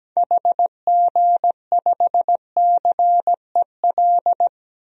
Morse code, heard as a keyed tone, about 17 wpm, 700 Hz. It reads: HG5CEL
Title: DXCC-100 Call Signs - 6 Characters - 17wpm